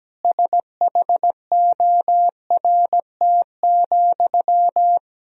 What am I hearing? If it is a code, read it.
Morse code, heard as a keyed tone, about 17 wpm, 700 Hz. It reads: SHORT,
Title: Sentences from Top 1000 Words - Review Part 2 - 17wpm